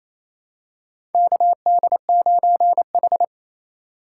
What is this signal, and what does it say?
Morse code, heard as a keyed tone, about 28 wpm, 700 Hz. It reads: KD9H
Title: USA Call Signs - 4 Characters - 4x Character Spacing - 28wpm